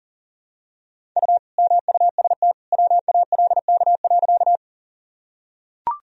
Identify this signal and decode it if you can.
Morse code, heard as a keyed tone, about 37 wpm, 700 Hz. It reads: U MUST WALK.
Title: Sentences from Top 300 Words and Abbreviations 40wpm